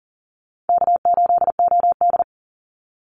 Morse code, heard as a keyed tone, about 40 wpm, 700 Hz. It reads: K8OD
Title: Call Signs - 40wpm